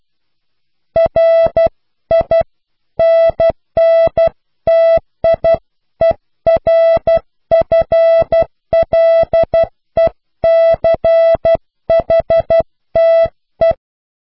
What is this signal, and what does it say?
Morse code, heard as a keyed tone, about 12 wpm, 660 Hz. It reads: RINNTIERFLECHTE
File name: cw / Pseudo woorden (15)